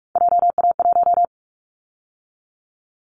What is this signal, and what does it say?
Morse code, heard as a keyed tone, about 44 wpm, 700 Hz. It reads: JA1
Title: Callsign Prefixes for Top 50 DXCC - 45wpm